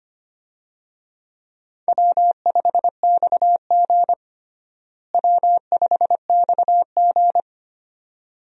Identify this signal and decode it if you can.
Morse code, heard as a keyed tone, about 25 wpm, 700 Hz. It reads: W5XG W5XG
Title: USA Call Signs - 4 Characters - Repeated 2x - 25wpm